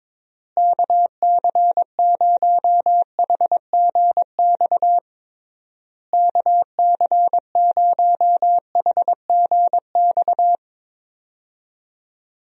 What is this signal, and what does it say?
Morse code, heard as a keyed tone, about 22 wpm, 700 Hz. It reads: KC0HGX KC0HGX
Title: USA Call Signs - 6 Characters - Repeated 3x - 22wpm